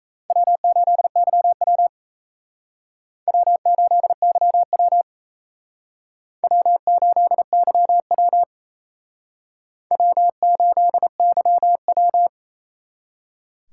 Morse code, heard as a keyed tone, about 36 wpm, 700 Hz. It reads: W8YW W8YW W8YW W8YW